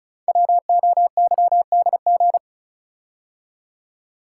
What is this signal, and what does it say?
Morse code, heard as a keyed tone, about 34 wpm, 690 Hz. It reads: WOYDG
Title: Code Groups - 5 Letters - 35wpm